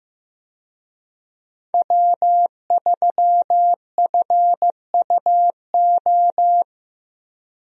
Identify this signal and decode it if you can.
Morse code, heard as a keyed tone, about 15 wpm, 700 Hz. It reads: W3FUO
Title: USA Call Signs - 5 Characters - Repeated 3x - 15wpm